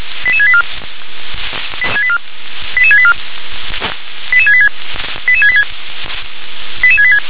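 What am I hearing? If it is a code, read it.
Five-tone selective call (ZVEI), heard as quick runs of five tones, tones of about 70 ms. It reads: 88684 88684 88686 88686 88686
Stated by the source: ZVEI five-tone call 88738 received by SDR.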